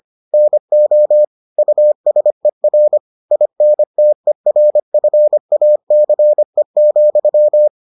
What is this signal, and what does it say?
Morse code, heard as a keyed tone, about 25 wpm, 600 Hz. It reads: NO USER INTERFACE,